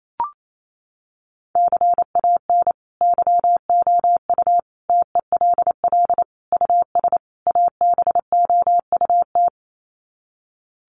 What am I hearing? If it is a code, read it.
Morse code, heard as a keyed tone, about 28 wpm, 700 Hz. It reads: CAN YOU TELL US ABOUT
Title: Sets of 5 Words Encore2 28wpm